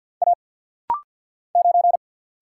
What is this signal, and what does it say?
Morse code, heard as a keyed tone, about 50 wpm, 700 Hz. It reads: A 9
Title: Single Letter-Number Alt - Rapid-Fire 50wpm